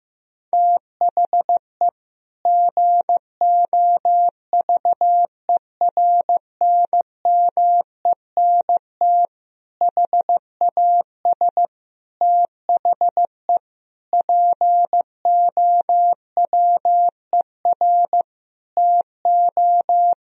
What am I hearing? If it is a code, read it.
Morse code, heard as a keyed tone, about 15 wpm, 700 Hz. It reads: THE GOVERNMENT HAS THE POWER TO
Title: Sentences from Top 1000 Words - Review Part 3 - 15wpm